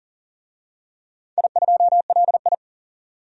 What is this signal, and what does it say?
Morse code, heard as a keyed tone, about 42 wpm, 700 Hz. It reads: I2LI
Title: DXCC-100 Call Signs - 4 Characters - 40wpm